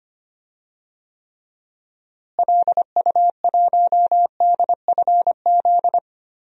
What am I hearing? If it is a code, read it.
Morse code, heard as a keyed tone, about 25 wpm, 700 Hz. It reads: LU1DFZ
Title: DXCC-100 Call Signs - 6 Characters - Repeated 2x - 25wpm